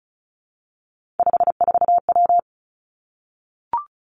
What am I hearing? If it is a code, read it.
Morse code, heard as a keyed tone, about 36 wpm, 700 Hz. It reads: 54W
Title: Three Challenging Letter-Number Combinations 35wpm